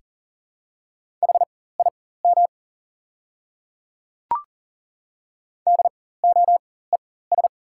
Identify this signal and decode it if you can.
Morse code, heard as a keyed tone, about 40 wpm, spaced as for 19 wpm, 690 Hz. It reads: HIM DOES
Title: 4x Character Spacing - Top 200 Words - 40wpm